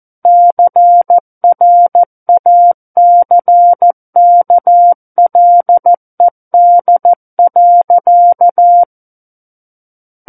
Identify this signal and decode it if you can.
Morse code, heard as a keyed tone, about 14 wpm, 700 Hz. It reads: CRACKLED.
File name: Book One / 05-Chapter 5 - 14 WPM